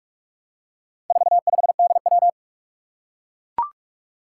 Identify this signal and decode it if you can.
Morse code, heard as a keyed tone, about 44 wpm, 700 Hz. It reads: 45DW
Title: Four Challenging Letter-Number Combinations 45wpm